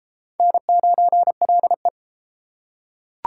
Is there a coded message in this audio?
Yes: Morse code from 0.4 to 1.9 s, heard as a keyed tone, about 32 wpm, 700 Hz. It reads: N9LE